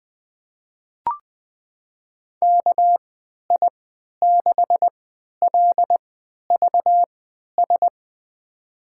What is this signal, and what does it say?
Morse code, heard as a keyed tone, about 21 wpm, spaced as for 11 wpm, 700 Hz. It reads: KI6LVS
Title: USA Call Signs - 3x Character Spacing - 20wpm